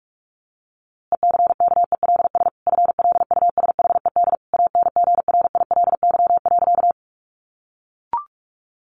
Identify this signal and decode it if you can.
Morse code, heard as a keyed tone, about 44 wpm, 700 Hz. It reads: ECKELS FLUSHED ANGRILY.
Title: A Sound Of Thunder 45wpm